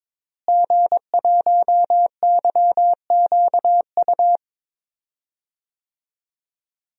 Morse code, heard as a keyed tone, about 22 wpm, 700 Hz. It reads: G1YQU
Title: DXCC-100 Call Signs - 5 Characters - Repeated 2x - 22wpm